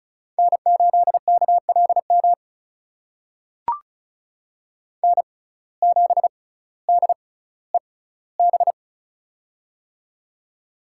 Morse code, heard as a keyed tone, about 36 wpm, spaced as for 18 wpm, 700 Hz. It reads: N8KLM N7DEB
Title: USA Call Signs - 5 Characters - 6x Character Spacing - 35wpm